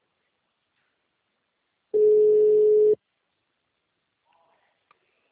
Ringback tone (Europe, 425 Hz).